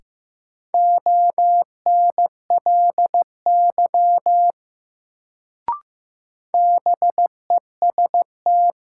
Morse code, heard as a keyed tone, about 15 wpm, 700 Hz. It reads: ONLY BEST
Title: Sets of 2 Words - New Word Drill - 15wpm